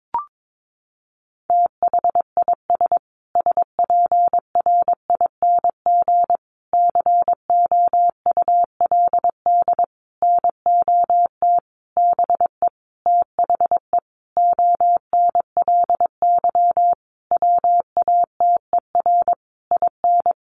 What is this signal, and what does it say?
Morse code, heard as a keyed tone, about 22 wpm, 700 Hz. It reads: THIS SPRING COULD NOT BE THE ONLY WATER IN
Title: Sentences from Top 1500 Words 22wpm